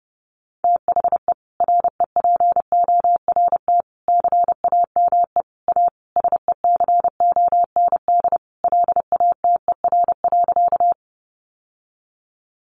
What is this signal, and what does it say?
Morse code, heard as a keyed tone, about 30 wpm, 700 Hz. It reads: THE REPORT CAME A SECOND LATER.
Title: Sentences from Top 900 Words 30wpm